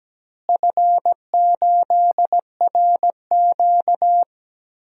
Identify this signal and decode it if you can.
Morse code, heard as a keyed tone, about 17 wpm, 700 Hz. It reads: F8RQ
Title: DXCC-100 Call Signs - 4 Characters - Repeated 3x - 17wpm